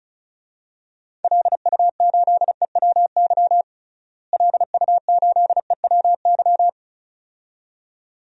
Morse code, heard as a keyed tone, about 36 wpm, 700 Hz. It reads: LU8EWY LU8EWY
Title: DXCC-100 Call Signs - 6 Characters - Repeated 2x - 35wpm